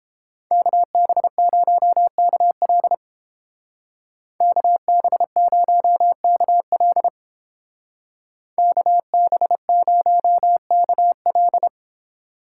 Morse code, heard as a keyed tone, about 29 wpm, 700 Hz. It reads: KB0KL KB0KL KB0KL